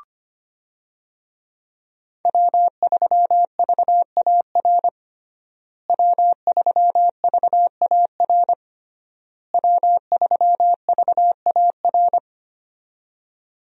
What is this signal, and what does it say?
Morse code, heard as a keyed tone, about 25 wpm, 700 Hz. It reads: W3VAR W3VAR W3VAR
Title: USA Call Signs - 5 Characters - Repeated 3x - 25wpm